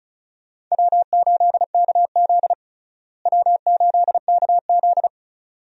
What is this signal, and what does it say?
Morse code, heard as a keyed tone, about 36 wpm, 700 Hz. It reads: W8KZ W8KZ
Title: USA Call Signs - 4 Characters - Repeated 2x - 35wpm